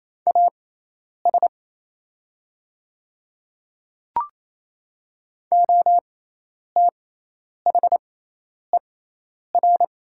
Morse code, heard as a keyed tone, about 28 wpm, spaced as for 10 wpm, 700 Hz. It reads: AS OTHER
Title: 6x Character Spacing - Top 100 Words - 28wpm